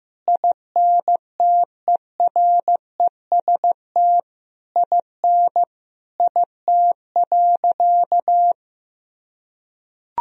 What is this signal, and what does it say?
Morse code, heard as a keyed tone, about 15 wpm, 700 Hz. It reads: INTEREST IN IT.
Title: Sentences from Top 1000 Words - Review Part 2 - 15wpm